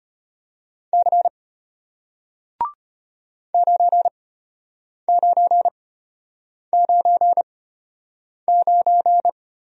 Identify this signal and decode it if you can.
Morse code, heard as a keyed tone, about 33 wpm, spaced as for 8 wpm, 700 Hz. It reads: C 9999